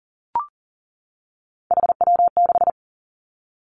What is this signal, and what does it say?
Morse code, heard as a keyed tone, about 38 wpm, 700 Hz. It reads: HW6
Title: Three Challenging Letter-Number Combinations 40wpm